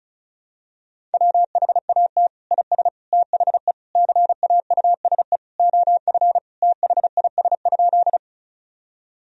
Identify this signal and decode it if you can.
Morse code, heard as a keyed tone, about 34 wpm, 700 Hz. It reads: WHAT IS THE CAUSE OF THIS?
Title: Sentences from Top 1000 Words - Review Part 1 - 35wpm